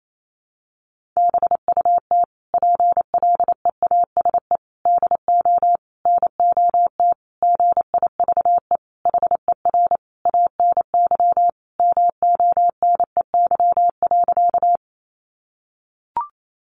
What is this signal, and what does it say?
Morse code, heard as a keyed tone, about 28 wpm, 700 Hz. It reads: BUT PLEASE DO NOT GIVE HER ANY MONEY.
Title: Sentences from Top 1500 Words 28wpm